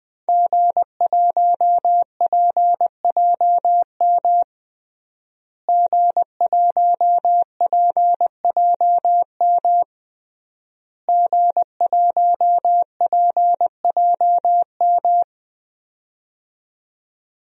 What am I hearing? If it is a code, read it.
Morse code, heard as a keyed tone, about 20 wpm, 700 Hz. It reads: G1PJM G1PJM G1PJM
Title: DXCC-100 Call Signs - 5 Characters - Repeated 3x - 20wpm